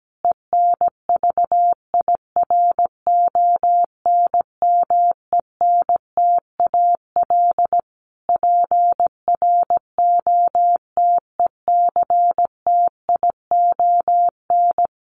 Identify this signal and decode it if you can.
Morse code, heard as a keyed tone, about 17 wpm, 700 Hz. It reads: ENVIRONMENTAL PROTECTION
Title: Sets of 5 Words Encore2 17wpm